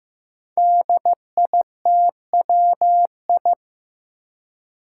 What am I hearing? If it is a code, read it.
Morse code, heard as a keyed tone, about 15 wpm, 700 Hz. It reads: DITWI
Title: Five Challenging Letter-Number Combinations 15wpm